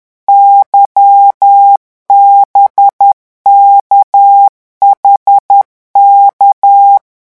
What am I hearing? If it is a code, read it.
Morse code, heard as a keyed tone, about 10 wpm, 800 Hz. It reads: YBKHK
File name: morse mp3 / revision5-vitesse 10 mots